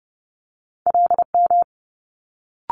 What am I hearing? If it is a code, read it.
Morse code, heard as a keyed tone, about 30 wpm, 700 Hz. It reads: LM
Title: Multi-letter Phonemes Quick Fire 30wpm